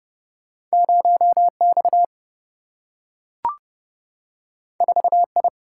Morse code, heard as a keyed tone, about 31 wpm, 700 Hz. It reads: 0 X 4 I
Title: Single-Letter Single-Number 30wpm